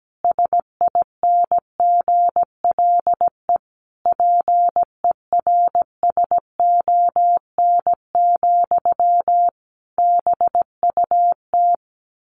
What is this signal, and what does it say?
Morse code, heard as a keyed tone, about 17 wpm, 700 Hz. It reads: SINGLE PERSON, BUT
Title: Sentences from Top 1400 Words 17wpm